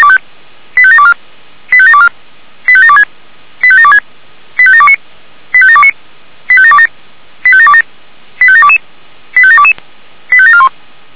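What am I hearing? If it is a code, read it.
Five-tone selective call (ZVEI), heard as quick runs of five tones, tones of about 70 ms. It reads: 85724 85724 85726 85726 85729 85729 85727 85727 85722 85722 85741
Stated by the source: ZVEI five-tone call 85721 received by SDR, 464 channel, Feuerwehr Orlamünde.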